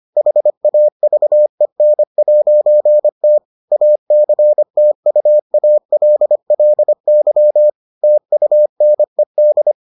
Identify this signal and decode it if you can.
Morse code, heard as a keyed tone, about 25 wpm, 600 Hz. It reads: HAVEN'T ACTUALLY TUNED